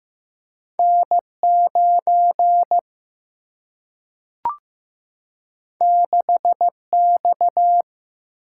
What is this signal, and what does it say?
Morse code, heard as a keyed tone, about 15 wpm, 700 Hz. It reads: N9 6X
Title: Single-Letter Single-Number 15wpm